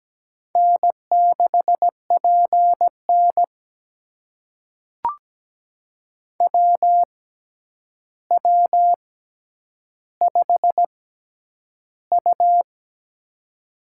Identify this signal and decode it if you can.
Morse code, heard as a keyed tone, about 17 wpm, spaced as for 9 wpm, 700 Hz. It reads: N6PN WW5U
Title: USA Call Signs - 4 Characters - 6x Character Spacing - 17wpm